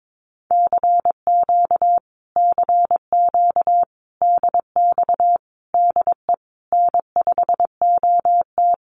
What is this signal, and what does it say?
Morse code, heard as a keyed tone, about 22 wpm, 700 Hz. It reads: CQ CQ DX DE N5OT
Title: Common QSO Phrases Encore 22wpm